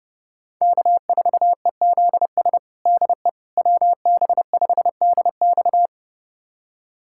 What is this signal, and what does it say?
Morse code, heard as a keyed tone, about 30 wpm, 700 Hz. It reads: K4EZS DE WB5DX
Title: USA Callsign DE Callsign - 5 Characters - 30wpm